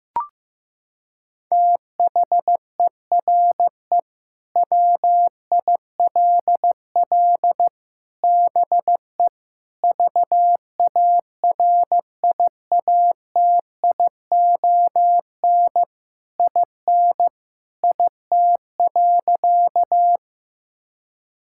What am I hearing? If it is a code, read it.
Morse code, heard as a keyed tone, about 16 wpm, 700 Hz. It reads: THERE WILL BE VARIATION IN IT.
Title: Sentences from Top 1600 Words 15wpm